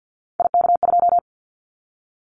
Morse code, heard as a keyed tone, about 50 wpm, 700 Hz. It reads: IK2
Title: Callsign Prefixes for Top 25 DXCC - 50wpm